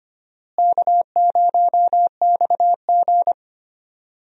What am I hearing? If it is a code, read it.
Morse code, heard as a keyed tone, about 25 wpm, 700 Hz. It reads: K0XG
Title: USA Call Signs - 4 Characters - Repeated 3x - 25wpm